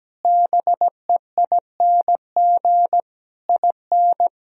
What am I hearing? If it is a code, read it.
Morse code, heard as a keyed tone, about 17 wpm, 700 Hz. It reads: BEING IN
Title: Sentences from Top 1000 Words - Review Part 3 - 17wpm